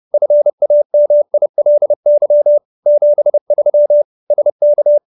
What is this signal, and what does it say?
Morse code, heard as a keyed tone, about 30 wpm, 590 Hz. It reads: FAMILY 73 SK